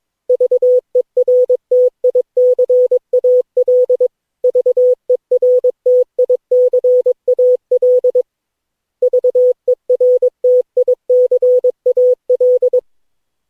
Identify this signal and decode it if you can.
Morse code, heard as a keyed tone, about 22 wpm, 500 Hz. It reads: VERTICAL VERTICAL VERTICAL
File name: cw / Session9 cw training